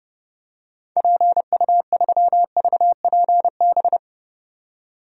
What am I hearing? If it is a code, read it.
Morse code, heard as a keyed tone, about 30 wpm, 700 Hz. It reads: PU3VPB
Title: DXCC-100 Call Signs - 6 Characters - 30wpm